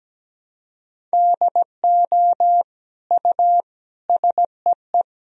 Morse code, heard as a keyed tone, about 17 wpm, 700 Hz. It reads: DO U SEE
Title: Sentences from Top 100 Words and Abbreviations 17wpm